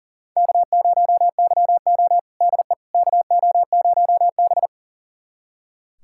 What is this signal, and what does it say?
Morse code, heard as a keyed tone, about 38 wpm, 700 Hz. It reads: K0YO DE KO0B
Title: USA Callsign DE Callsign - 4 Characters - 40wpm